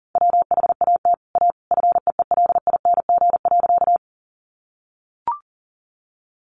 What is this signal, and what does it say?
Morse code, heard as a keyed tone, about 40 wpm, 700 Hz. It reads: WHAT A FEELING.
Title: Sentences from Top 1200 Words 40wpm